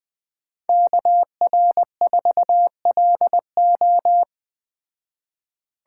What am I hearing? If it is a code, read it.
Morse code, heard as a keyed tone, about 20 wpm, 700 Hz. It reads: KR4LO
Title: Speed Racing - USA 2x2 Call Signs 17wpm